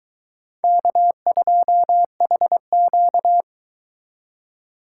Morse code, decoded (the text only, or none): K2HQ